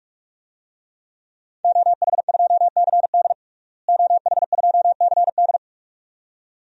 Morse code, heard as a keyed tone, about 47 wpm, 700 Hz. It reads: OH2CD OH2CD
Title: DXCC-100 Call Signs - 5 Characters - Repeated 2x - 45wpm